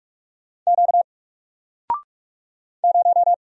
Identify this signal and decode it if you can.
Morse code, heard as a keyed tone, about 45 wpm, 700 Hz. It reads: Q 0
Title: Single Character (Letter:Number:Punctuation:Prosign) 45wpm